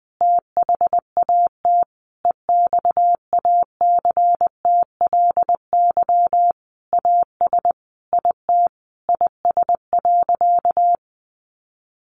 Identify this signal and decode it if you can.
Morse code, heard as a keyed tone, about 20 wpm, 700 Hz. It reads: THAT EXACTLY AS IT IS.
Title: Sentences from Top 900 Words 20wpm